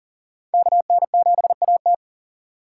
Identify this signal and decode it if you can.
Morse code, heard as a keyed tone, about 40 wpm, 700 Hz. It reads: KN7AT
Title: USA Call Signs - 3x Character Spacing - 40wpm